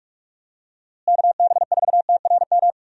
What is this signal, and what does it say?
Morse code, heard as a keyed tone, about 43 wpm, 700 Hz. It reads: KB4TRM
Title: USA Call Signs - 6 Characters - Repeated 2x - 45wpm